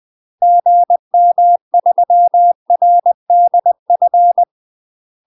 Morse code, heard as a keyed tone, about 20 wpm, 700 Hz. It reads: GM3RDF
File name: UK MASTER QSOs TRAINING / 20WPM / UK MASTER QSOs 0029